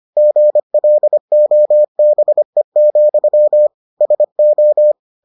Morse code, heard as a keyed tone, about 25 wpm, 600 Hz. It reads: GLOBE, SO